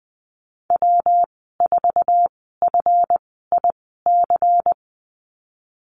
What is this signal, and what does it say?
Morse code, heard as a keyed tone, about 19 wpm, spaced as for 14 wpm, 700 Hz. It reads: W4FIC
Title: 173 Lesson 28g - Number 9 - Call Signs 20wpm